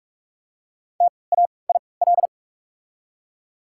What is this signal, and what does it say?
Morse code, heard as a keyed tone, about 46 wpm, spaced as for 25 wpm, 700 Hz. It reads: TAIL
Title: 3x Character Spacing - Top 500 Words - 45wpm